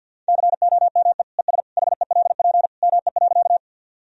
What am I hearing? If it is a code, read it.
Morse code, heard as a keyed tone, about 51 wpm, 700 Hz. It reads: COME ES HELP ME.
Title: Sentences from Top 200 Words and Abbreviations 50wpm